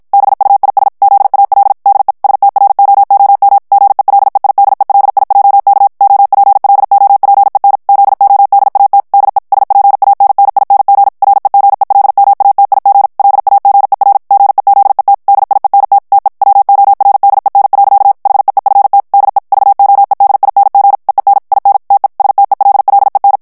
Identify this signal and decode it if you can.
Morse code, heard as a keyed tone, about 51 wpm, 800 Hz. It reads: BUIS ZAL DE STROOM GELEIDELIJK OPLOPEN ZODAT DE SPANNING REGELMATIG LAGER GEZET DIENT TE WORDEN. HEEFT DE VOEDING EEN IN TE STELLEN
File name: HOLLANDAIS texte clair 2 vitesse 50 mots minute